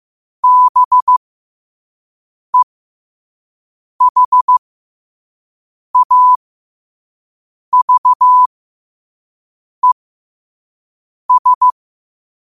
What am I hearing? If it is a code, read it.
Morse code, heard as a keyed tone, about 15 wpm, spaced as for 5 wpm, 1000 Hz. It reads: BEHAVES
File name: qotdfiles / 5wpm